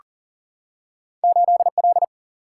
Morse code, heard as a keyed tone, about 40 wpm, 700 Hz. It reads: 8L
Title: Single-Letter Single-Number 40wpm